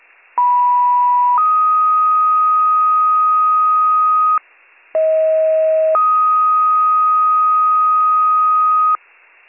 Two-tone page, heard as a one-second tone, then a three-second tone, about 980 then 1250 Hz, about 640 then 1250 Hz. It is heard from 0.4 to 9.0 s.